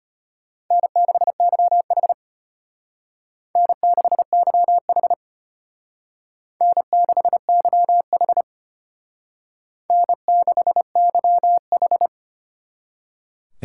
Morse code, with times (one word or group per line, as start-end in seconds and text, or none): 0.7-2.1 N6YH
3.6-5.1 N6YH
6.6-8.4 N6YH
9.9-12.1 N6YH